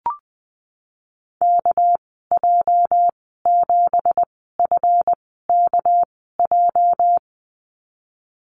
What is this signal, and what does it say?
Morse code, heard as a keyed tone, about 20 wpm, spaced as for 15 wpm, 700 Hz. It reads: KJ7FKJ
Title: 231 Lesson 37g - Letter x - Call Signs 20wpm